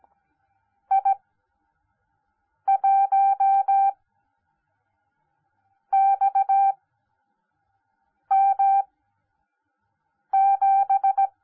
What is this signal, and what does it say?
Morse code, heard as a keyed tone, about 17 wpm, spaced as for 5 wpm, 780 Hz. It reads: I1XM7
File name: Onlinelehrgang / morsen / DJ4UF Morsen10